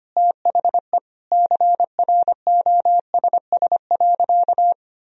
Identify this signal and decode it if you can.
Morse code, heard as a keyed tone, about 25 wpm, 700 Hz. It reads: THE CROSS.
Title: Sentences from Top 1000 Words - Review Part 1 - 25wpm